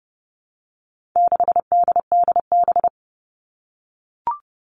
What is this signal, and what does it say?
Morse code, heard as a keyed tone, about 30 wpm, 700 Hz. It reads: 6DDB
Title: All Cut Numbers - 4-Digit 30wpm